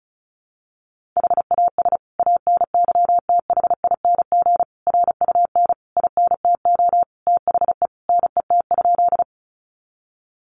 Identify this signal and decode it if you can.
Morse code, heard as a keyed tone, about 34 wpm, 700 Hz. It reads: HAS ANYTHING RUN INTO THE NET?
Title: Sentences from Top 1400 Words 35wpm